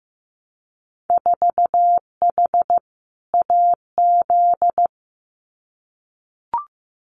Morse code, heard as a keyed tone, about 15 wpm, 700 Hz. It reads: 4H AZ
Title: Winter Field Day Exchanges Encore 15wpm